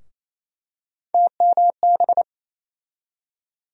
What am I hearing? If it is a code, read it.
Morse code, heard as a keyed tone, about 27 wpm, 700 Hz. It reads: TMB
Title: USA Call Sign Postfixes - 3 Letters 28wpm